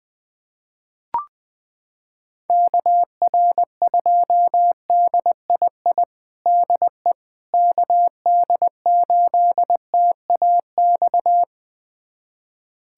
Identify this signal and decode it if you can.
Morse code, heard as a keyed tone, about 21 wpm, 700 Hz. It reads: KR2DII DE KD8TAX
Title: USA Callsign DE Callsign 20wpm